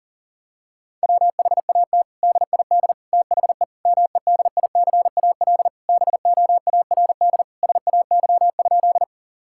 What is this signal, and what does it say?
Morse code, heard as a keyed tone, about 40 wpm, 700 Hz. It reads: WHAT DID THE MEDICAL BOARD SAY?
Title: Sentences from Top 1000 Words - Review Part 5 - 40wpm